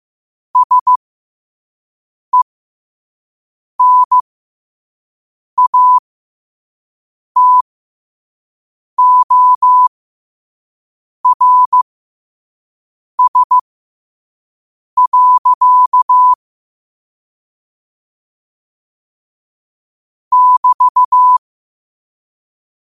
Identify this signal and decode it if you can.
Morse code, heard as a keyed tone, about 15 wpm, spaced as for 5 wpm, 1000 Hz. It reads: SENATORS. <BT>
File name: qotdfiles / 5wpm